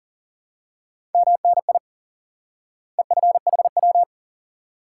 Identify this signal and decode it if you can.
Morse code, heard as a keyed tone, about 40 wpm, 700 Hz. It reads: MNI EFHW
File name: IQE Course 3 - Lesson 89 - YAGI - Two Elements - 40wpm